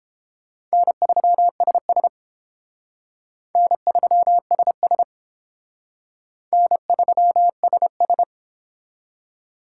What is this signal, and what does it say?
Morse code, heard as a keyed tone, about 28 wpm, 700 Hz. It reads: N3SS N3SS N3SS